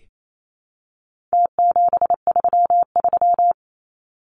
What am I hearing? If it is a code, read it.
Morse code, heard as a keyed tone, about 28 wpm, 700 Hz. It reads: T733
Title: Numbers - Contest Style - 4-Digit 28wpm